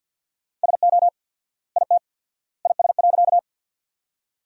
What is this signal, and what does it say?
Morse code, heard as a keyed tone, about 50 wpm, 700 Hz. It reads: SO IT IS.
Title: 4x Word-Spacing - Sentences from Top 100 Words - 50wpm